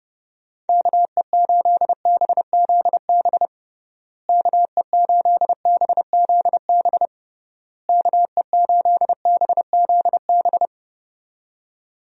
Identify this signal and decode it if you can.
Morse code, heard as a keyed tone, about 32 wpm, 700 Hz. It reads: KE8BZB KE8BZB KE8BZB
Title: USA Call Signs - 6 Characters - Repeated 3x - 30wpm